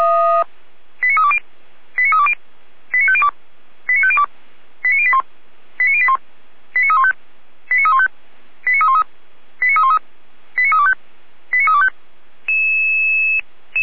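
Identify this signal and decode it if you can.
Five-tone selective call (ZVEI), heard as quick runs of five tones, tones of about 70 ms. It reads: 89320 89320 89502 89502 89081 89081 89315 89315 89313 89313 89325 89325
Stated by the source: ZVEI five-tone selective call received on SDR by feuerwehr-orlamuende.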